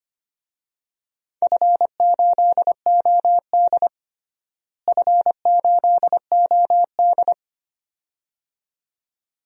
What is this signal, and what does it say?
Morse code, heard as a keyed tone, about 25 wpm, 700 Hz. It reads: F8OD F8OD
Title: DXCC-100 Call Signs - 4 Characters - Repeated 2x - 25wpm